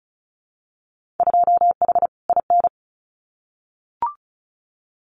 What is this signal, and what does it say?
Morse code, heard as a keyed tone, about 34 wpm, 700 Hz. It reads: 2H IN
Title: Winter Field Day Exchanges Encore 35wpm